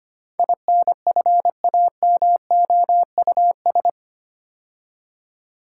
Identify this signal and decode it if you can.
Morse code, heard as a keyed tone, about 25 wpm, 700 Hz. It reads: INFAMOUS
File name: 8 Letter Words - First 3 Letters Given - 25wpm